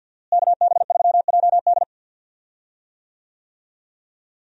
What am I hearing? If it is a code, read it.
Morse code, heard as a keyed tone, about 50 wpm, 700 Hz. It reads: KB3JD
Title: USA Call Signs - 5 Characters - Repeated 3x - 50wpm